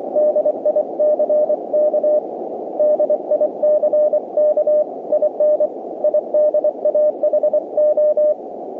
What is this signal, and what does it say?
Morse code, heard as a keyed tone, about 24 wpm, 590 Hz. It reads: DICK DICK IN IDAHO